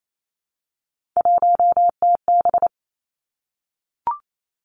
Morse code, heard as a keyed tone, about 28 wpm, 700 Hz. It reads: 1TB
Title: Three Challenging Letter-Number Combinations 28wpm